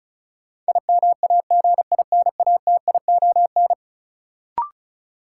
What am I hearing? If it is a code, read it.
Morse code, heard as a keyed tone, about 35 wpm, 700 Hz. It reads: IMAGINATION
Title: Sentences from Top 1300 Words - New Word Drill Alt - 35wpm